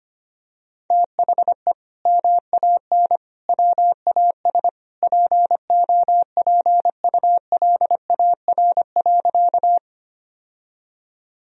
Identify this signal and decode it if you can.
Morse code, heard as a keyed tone, about 25 wpm, 700 Hz. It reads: THE MAN WAS POPULAR.
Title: Sentences from Top 1000 Words - Review Part 3 - 25wpm